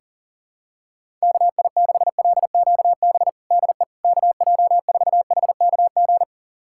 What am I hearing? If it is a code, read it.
Morse code, heard as a keyed tone, about 40 wpm, 700 Hz. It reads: KI6LQB DE KJ4HKG
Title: USA Callsign DE Callsign - 6 Characters - 40wpm